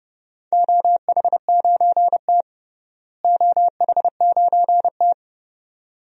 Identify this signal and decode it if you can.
Morse code, heard as a keyed tone, about 30 wpm, 700 Hz. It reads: OH9T OH9T
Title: DXCC-100 Call Signs - 4 Characters - Repeated 2x - 30wpm